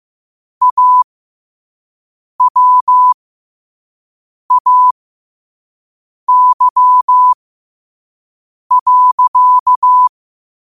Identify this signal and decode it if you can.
Morse code, heard as a keyed tone, about 15 wpm, spaced as for 5 wpm, 1000 Hz. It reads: AWAY.
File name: qotdfiles / 5wpm